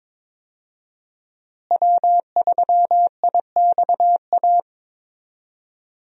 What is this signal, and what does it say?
Morse code, heard as a keyed tone, about 22 wpm, 700 Hz. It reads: W3IXA
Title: USA Call Signs - 5 Characters - Repeated 3x - 22wpm